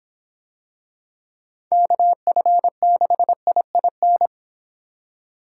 Morse code, heard as a keyed tone, about 26 wpm, 700 Hz. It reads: KF6IIN